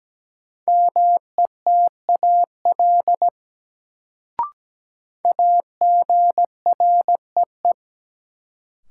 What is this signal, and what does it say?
Morse code, heard as a keyed tone, about 17 wpm, 700 Hz. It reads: METAL AGREE
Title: Sentences from Top 1000 Words - Review Part 6 - New Word Drill - 17wpm